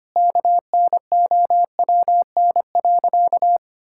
Morse code, heard as a keyed tone, about 25 wpm, 700 Hz. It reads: KNOWN.
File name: practice / 25wpm / Sentences with 3-Word Phrases - 25wpm